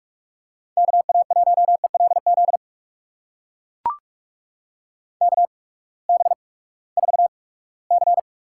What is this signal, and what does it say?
Morse code, heard as a keyed tone, about 43 wpm, spaced as for 22 wpm, 700 Hz. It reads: KA1ELZ KB4C
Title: USA Call Signs - 8x Character Spacing - 45wpm